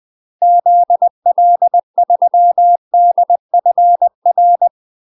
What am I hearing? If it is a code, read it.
Morse code, heard as a keyed tone, about 20 wpm, 700 Hz. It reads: ZL3DFR
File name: UK MASTER QSOs TRAINING / 20WPM / UK MASTER QSOs 0028